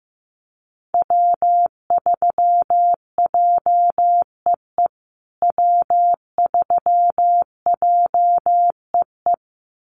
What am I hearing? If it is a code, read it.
Morse code, heard as a keyed tone, about 15 wpm, 700 Hz. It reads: W3JEE W3JEE
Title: USA Call Signs Repeated 2x 15wpm